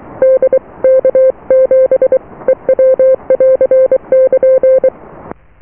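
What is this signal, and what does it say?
Morse code, heard as a keyed tone, about 23 wpm, spaced as for 20 wpm, 520 Hz. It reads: DK7EW <AR> <KN>
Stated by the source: Morse QSO on 14 MHz with a German station, received at RST 579.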